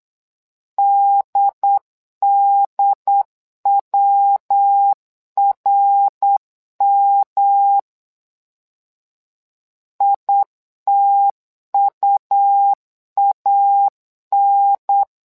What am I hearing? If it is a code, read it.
Morse code, heard as a keyed tone, about 8 wpm, 800 Hz. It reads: DDWRM ITUAN
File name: LEZIONI / 3 MWRO / 3 MWRO 8wpm 2606